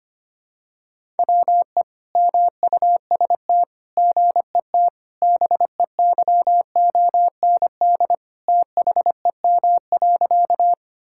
Morse code, heard as a keyed tone, about 25 wpm, 700 Hz. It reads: WE MUST GET BEYOND THEM.
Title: Sentences from Top 1000 Words - Review Part 6 - 25wpm